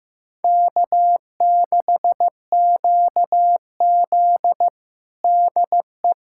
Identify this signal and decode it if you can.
Morse code, heard as a keyed tone, about 15 wpm, 700 Hz. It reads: K6QZ DE
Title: USA Callsign DE Callsign - 4 Characters - 15wpm